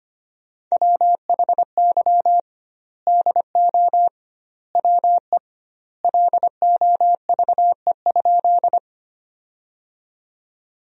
Morse code, heard as a keyed tone, about 25 wpm, 700 Hz. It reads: WHY DO WE LOVE?
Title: 2x Word-Spacing - Sentences from Top 400 Words - 25wpm